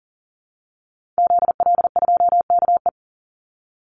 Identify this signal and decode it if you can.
Morse code, heard as a keyed tone, about 40 wpm, 700 Hz. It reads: ZL2KE
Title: Call Signs Repeated 3x - 40wpm